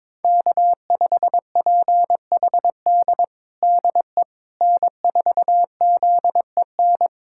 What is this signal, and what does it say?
Morse code, heard as a keyed tone, about 22 wpm, 700 Hz. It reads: K5PHD DE N4ZEN
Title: USA Callsign DE Callsign - 5 Characters - 22wpm